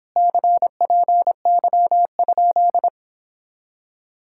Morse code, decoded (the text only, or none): CPY?